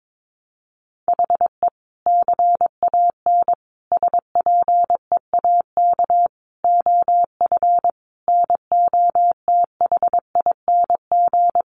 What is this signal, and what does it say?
Morse code, heard as a keyed tone, about 22 wpm, 700 Hz. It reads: HE CAN SPEAK OF NOTHING
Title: Sentences from Top 1100 Words 22wpm